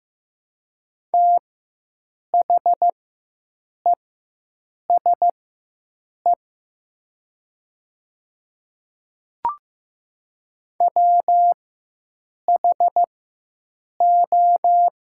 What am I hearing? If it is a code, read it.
Morse code, heard as a keyed tone, about 15 wpm, spaced as for 7 wpm, 700 Hz. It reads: THESE WHO
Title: 4x Character Spacing - Top 100 Words - 15wpm